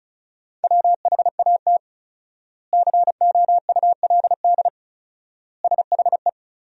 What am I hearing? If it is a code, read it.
Morse code, heard as a keyed tone, about 36 wpm, 700 Hz. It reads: WHAT COULD SHE
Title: 4x Word-Spacing - Sentences from Top 200 Words - 35wpm